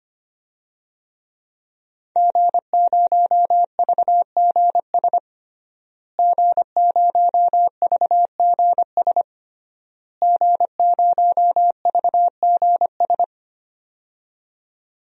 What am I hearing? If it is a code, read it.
Morse code, heard as a keyed tone, about 25 wpm, 700 Hz. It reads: G0VGS G0VGS G0VGS
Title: DXCC-100 Call Signs - 5 Characters - Repeated 3x - 25wpm